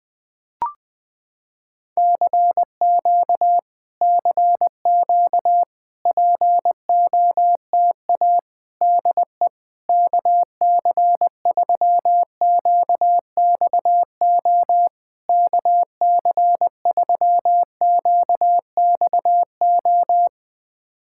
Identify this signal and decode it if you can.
Morse code, heard as a keyed tone, about 20 wpm, 700 Hz. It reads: CQ CQ POTA DE KC3QXO KC3QXO
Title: POTA QSO Practice - 20wpm